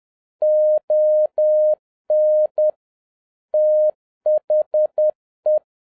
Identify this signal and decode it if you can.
Morse code, heard as a keyed tone, about 10 wpm, 610 Hz. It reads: ON THE